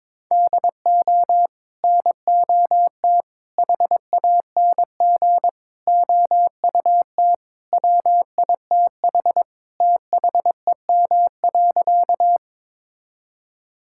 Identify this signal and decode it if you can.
Morse code, heard as a keyed tone, about 22 wpm, 700 Hz. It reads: DO NOT HANG OUT WITH THEM.
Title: Sentences from Top 1600 Words 22wpm